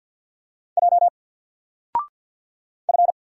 Morse code, heard as a keyed tone, about 50 wpm, 700 Hz. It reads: J F
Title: Single Letter-Number Alt 50wpm